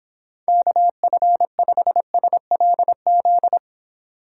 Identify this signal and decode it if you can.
Morse code, heard as a keyed tone, about 26 wpm, 700 Hz. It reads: KF5SLZ